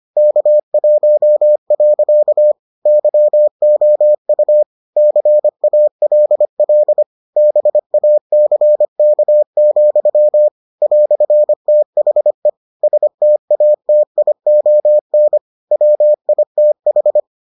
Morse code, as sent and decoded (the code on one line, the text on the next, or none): -.- .---- .-.-.- / -.-- --- ..- / -.-. .- .-.. .-.. / -... .- -.-. -.- --..-- / .-..-. - .... / ... - .- - .. --- -. / .-- .. - ....
K1. YOU CALL BACK, "THE STATION WITH